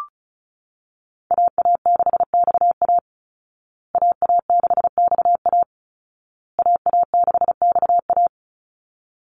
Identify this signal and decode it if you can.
Morse code, heard as a keyed tone, about 34 wpm, 700 Hz. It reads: AA6XA AA6XA AA6XA
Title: Call Signs Repeated 2x - 35wpm